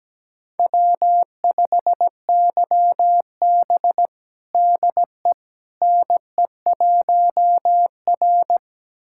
Morse code, heard as a keyed tone, about 17 wpm, 700 Hz. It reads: W5YB DE NE1R
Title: USA Callsign DE Callsign - 4 Characters - 17wpm